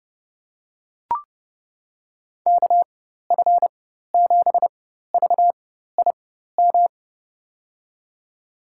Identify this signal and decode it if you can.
Morse code, heard as a keyed tone, about 31 wpm, spaced as for 14 wpm, 700 Hz. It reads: KF7VIM
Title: USA Call Signs - 6 Characters - 4x Character Spacing - 30wpm